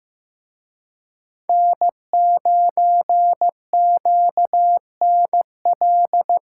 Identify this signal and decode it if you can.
Morse code, heard as a keyed tone, about 18 wpm, 700 Hz. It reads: N9QNL
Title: Speed Racing - USA Call Signs 15wpm